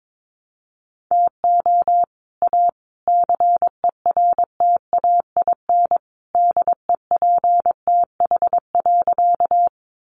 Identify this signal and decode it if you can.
Morse code, heard as a keyed tone, about 22 wpm, 700 Hz. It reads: TO A CERTAIN DEPTH.
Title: Sentences from Top 1200 Words 22wpm